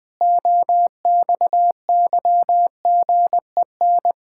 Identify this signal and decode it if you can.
Morse code, heard as a keyed tone, about 21 wpm, 700 Hz. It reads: OXYGEN
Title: Pangram Challenge v2 - 20wpm